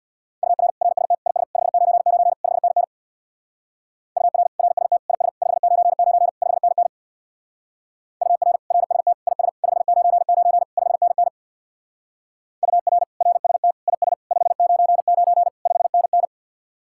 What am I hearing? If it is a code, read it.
Morse code, heard as a keyed tone, about 65 wpm, 700 Hz. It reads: UR RST IS 599 5NN UR RST IS 599 5NN UR RST IS 599 5NN UR RST IS 599 5NN